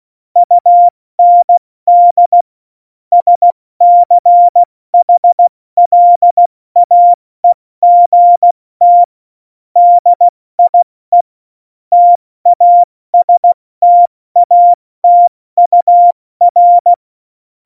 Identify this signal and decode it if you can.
Morse code, heard as a keyed tone, about 16 wpm, 700 Hz. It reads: UND SCHLAEGT DIE TASTATUR
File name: Morsekurs / Witze / Witz 22. Januar 2019, 70 F80